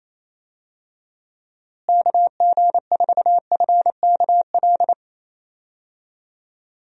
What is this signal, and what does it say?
Morse code, heard as a keyed tone, about 28 wpm, 700 Hz. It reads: KG4FKL
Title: USA Call Signs - 6 Characters - Repeated 2x - 28wpm